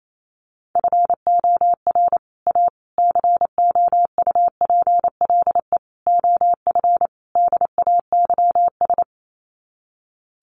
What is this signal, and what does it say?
Morse code, heard as a keyed tone, about 28 wpm, 700 Hz. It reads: FOR A COUPLE OF DAYS
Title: Sets of 5 Words Encore2 28wpm